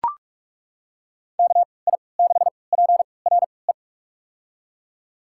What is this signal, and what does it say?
Morse code, heard as a keyed tone, about 42 wpm, spaced as for 25 wpm, 700 Hz. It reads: KI6PRE
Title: USA Call Signs - 6 Characters - 3x Character Spacing - 45wpm